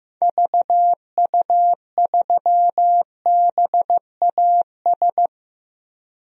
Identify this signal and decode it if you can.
Morse code, heard as a keyed tone, about 16 wpm, 700 Hz. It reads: VU3BAS
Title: DXCC-100 Call Signs - 6 Characters - Repeated 2x - 15wpm